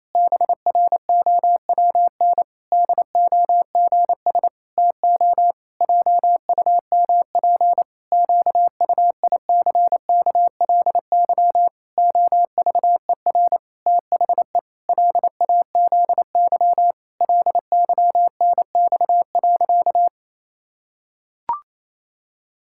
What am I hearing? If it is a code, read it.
Morse code, heard as a keyed tone, about 28 wpm, 700 Hz. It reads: BROWN DOGS TO JUMP QUICKLY OVER THE LAZY LYNX.
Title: Pangram Challenge v2 - 28wpm